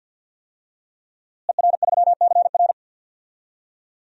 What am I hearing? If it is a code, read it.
Morse code, heard as a keyed tone, about 53 wpm, 700 Hz. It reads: ER3CR
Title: DXCC-100 Call Signs - 5 Characters - Repeated 2x - 50wpm